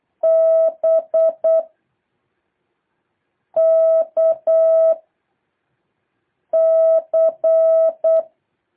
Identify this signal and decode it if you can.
Morse code, heard as a keyed tone, about 8 wpm, 650 Hz. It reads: B K C
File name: jh8cbh / 2023cw